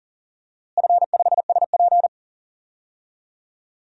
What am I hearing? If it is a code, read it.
Morse code, heard as a keyed tone, about 40 wpm, 700 Hz. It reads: F5SP
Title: DXCC-100 Call Signs - 4 Characters - Repeated 3x - 40wpm